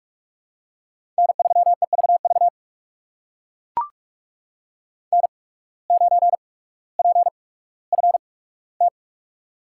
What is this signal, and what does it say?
Morse code, heard as a keyed tone, about 42 wpm, spaced as for 19 wpm, 700 Hz. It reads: N3EVV N9PFT
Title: USA Call Signs - 5 Characters - 8x Character Spacing - 45wpm